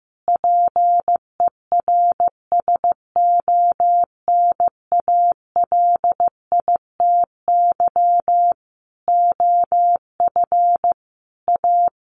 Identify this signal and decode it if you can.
Morse code, heard as a keyed tone, about 15 wpm, 700 Hz. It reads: PERSONALITY OF A
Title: Sentences from Top 1400 Words 15wpm